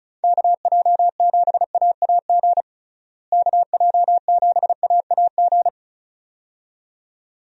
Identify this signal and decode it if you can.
Morse code, heard as a keyed tone, about 34 wpm, 700 Hz. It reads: KJ7AAG KJ7AAG
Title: USA Call Signs - 6 Characters - Repeated 2x - 35wpm